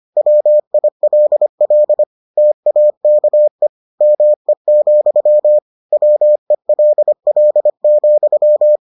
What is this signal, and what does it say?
Morse code, heard as a keyed tone, about 25 wpm, 600 Hz. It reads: WILL TAKE ME, WELL,